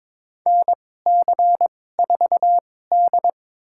Morse code, heard as a keyed tone, about 22 wpm, spaced as for 16 wpm, 700 Hz. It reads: NC4D
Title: USA Call Signs - 4 Characters - 2x Character Spacing - 22wpm